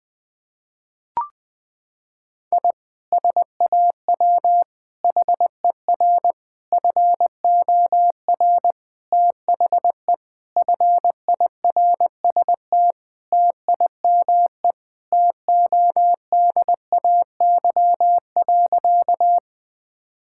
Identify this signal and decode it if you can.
Morse code, heard as a keyed tone, about 20 wpm, 700 Hz. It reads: I SAW HER FOR THE FIRST TIME TODAY.
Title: Sentences with 4-Word Phrases - 20wpm